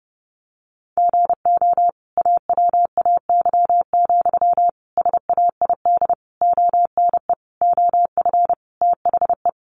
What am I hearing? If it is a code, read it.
Morse code, heard as a keyed tone, about 30 wpm, 700 Hz. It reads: GO AWAY, SAID ONE OF THE
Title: Sentences from Top 1500 Words 30wpm